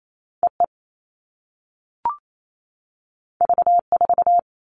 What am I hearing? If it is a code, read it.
Morse code, heard as a keyed tone, about 28 wpm, 700 Hz. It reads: EE V4
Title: All Cut Numbers - 2-Digit 28wpm